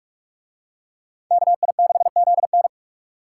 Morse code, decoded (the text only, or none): KI6ZN